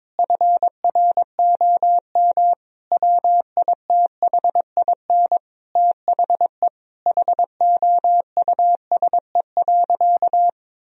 Morse code, heard as a keyed tone, about 22 wpm, 700 Hz. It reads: FROM WITHIN THE HOUSE.
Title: Sentences from Top 1000 Words - Review Part 4 - 22wpm